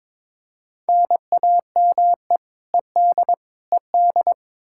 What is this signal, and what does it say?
Morse code, heard as a keyed tone, about 22 wpm, 700 Hz. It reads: NAME ED ED
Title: Practice QSO - Exchange 1 - Beginning - 22wpm